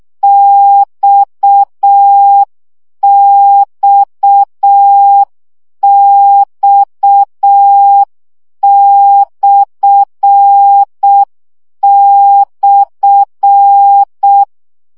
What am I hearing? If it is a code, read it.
Morse code, heard as a keyed tone, about 6 wpm, 800 Hz. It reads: XXX//
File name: CW gyakorlas x per 5 baud